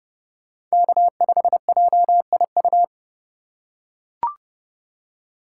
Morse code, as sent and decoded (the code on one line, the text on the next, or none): -.- ..... .--- .. ..-
K5JIU